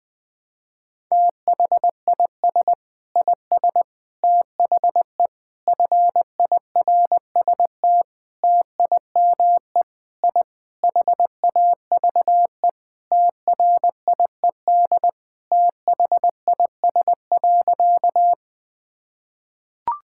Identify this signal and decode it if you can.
Morse code, heard as a keyed tone, about 20 wpm, 700 Hz. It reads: THIS IS THE FIRST TIME I HAVE TRIED THIS.
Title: Sentences with 4-Word Phrases - 20wpm